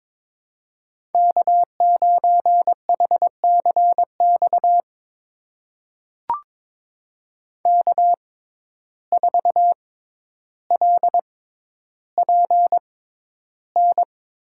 Morse code, heard as a keyed tone, about 22 wpm, spaced as for 11 wpm, 700 Hz. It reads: K9HCX K4LPN
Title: USA Call Signs - 5 Characters - 6x Character Spacing - 22wpm